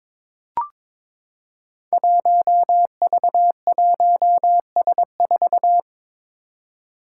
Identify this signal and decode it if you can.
Morse code, heard as a keyed tone, about 23 wpm, 700 Hz. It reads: 1V1S4
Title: Five Challenging Letter-Number Combinations 22wpm